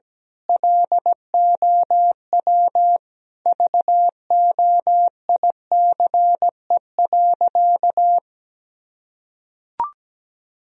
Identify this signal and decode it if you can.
Morse code, heard as a keyed tone, about 17 wpm, 700 Hz. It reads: LOW VOICE.
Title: Sentences from Top 1000 Words - Review Part 1 - 17wpm